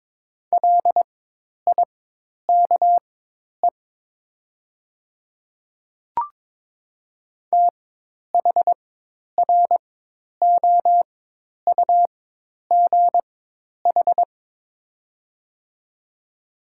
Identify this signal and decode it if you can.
Morse code, heard as a keyed tone, about 22 wpm, spaced as for 10 wpm, 700 Hz. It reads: LIKE THROUGH
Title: 4x Character Spacing - Top 200 Words - 22wpm